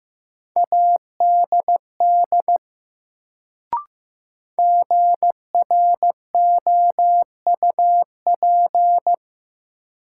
Morse code, heard as a keyed tone, about 15 wpm, 700 Hz. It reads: ADD GROUP
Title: Sentences from Top 1000 Words - Review Part 1 - New Word Drill - 15wpm